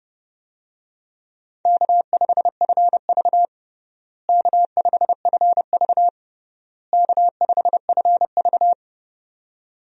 Morse code, heard as a keyed tone, about 30 wpm, 700 Hz. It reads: K5FV K5FV K5FV
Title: USA Call Signs - 4 Characters - Repeated 3x - 30wpm